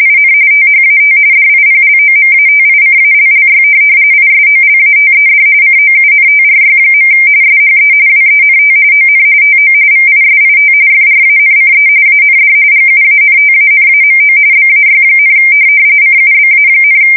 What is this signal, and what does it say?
RTTY, about 45.45 baud. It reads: IN THAT DIPLOMATICALLY ISOLATED COUNTRY. SHE WENT ON TO VISIT FRANCE AND GUANTANAMO BAY, CUBA,
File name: fsk170 / USS-New-Jersey